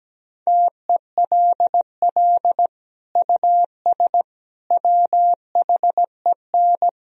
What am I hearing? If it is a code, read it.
Morse code, heard as a keyed tone, about 17 wpm, 700 Hz. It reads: TELL US WHEN
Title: Sentences from Top 1000 Words - Review Part 2 - 17wpm